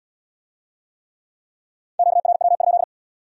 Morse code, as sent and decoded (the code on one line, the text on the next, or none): --.- .-. .-- ..--..
QRW?